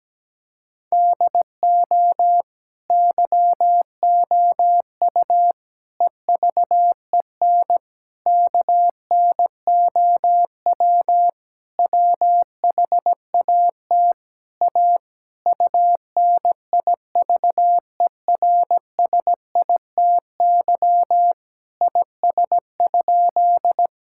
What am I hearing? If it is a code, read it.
Morse code, heard as a keyed tone, about 17 wpm, 700 Hz. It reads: DO YOU EVEN KNOW WHAT A UNIVERSITY IS?
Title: Sentences from Top 1000 Words - Review Part 5 - 17wpm